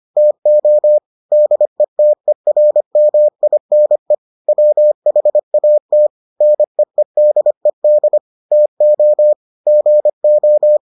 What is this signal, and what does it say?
Morse code, heard as a keyed tone, about 25 wpm, 600 Hz. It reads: TO DETERMINE WHAT NEEDED TO GO